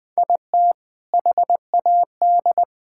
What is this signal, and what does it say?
Morse code, heard as a keyed tone, about 20 wpm, 700 Hz. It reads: IT HAD
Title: Sentences with 5-Word Phrases - Spoken Beginning - 20wpm